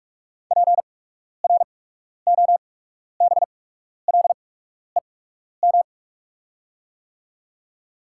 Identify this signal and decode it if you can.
Morse code, heard as a keyed tone, about 45 wpm, spaced as for 12 wpm, 690 Hz. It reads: PROBLEM
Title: 8x Character Spacing - Top 500 Words - 45wpm